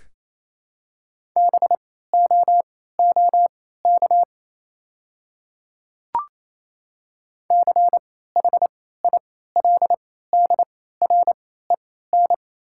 Morse code, heard as a keyed tone, about 28 wpm, spaced as for 16 wpm, 700 Hz. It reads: BOOK CHILDREN
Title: 3x Character Spacing - Top 300 Words - 28wpm